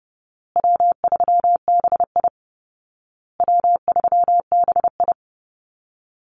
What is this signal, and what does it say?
Morse code, heard as a keyed tone, about 30 wpm, 700 Hz. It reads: W3BI W3BI
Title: Easy Call Signs Repeated 3x - 30wpm